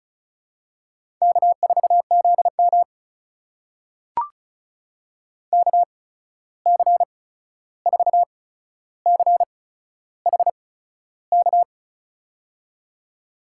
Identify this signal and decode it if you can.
Morse code, heard as a keyed tone, about 34 wpm, spaced as for 13 wpm, 700 Hz. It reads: K4ZM KC4CHK
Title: USA Call Signs - 8x Character Spacing - 35wpm